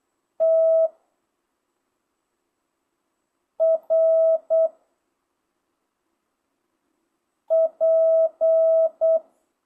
Morse code, heard as a keyed tone, about 8 wpm, 660 Hz. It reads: T R P
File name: jh8cbh / 2025cw